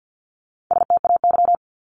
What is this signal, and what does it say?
Morse code, heard as a keyed tone, about 50 wpm, 700 Hz. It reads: STAY
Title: Top 500 Words Repeated 50wpm